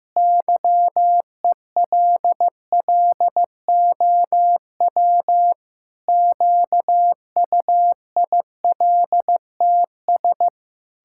Morse code, heard as a keyed tone, about 15 wpm, 700 Hz. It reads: YELLOW QUILTS
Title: Pangram Challenge v2 - 15wpm